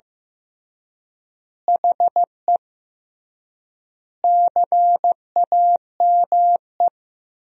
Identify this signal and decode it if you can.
Morse code, heard as a keyed tone, about 15 wpm, 700 Hz. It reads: HE CAME
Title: 3x Word-Spacing - Sentences from Top 500 Words - 15wpm